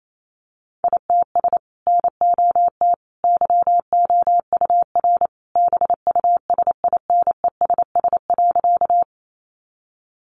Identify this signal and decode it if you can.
Morse code, heard as a keyed tone, about 28 wpm, 700 Hz. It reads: ITS NOT YOUR BUSINESS.
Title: Sentences from Top 600 Words 28wpm